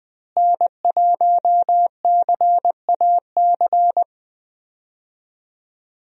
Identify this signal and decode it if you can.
Morse code, heard as a keyed tone, about 20 wpm, 700 Hz. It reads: N1CAC
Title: USA Call Signs - 5 Characters - Repeated 2x - 20wpm